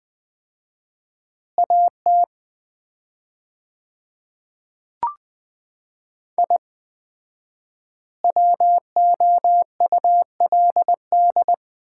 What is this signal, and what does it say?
Morse code, heard as a keyed tone, about 20 wpm, 700 Hz. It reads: AT I WOULD
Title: 4x Word-Spacing - Sets of 3 Words - 20wpm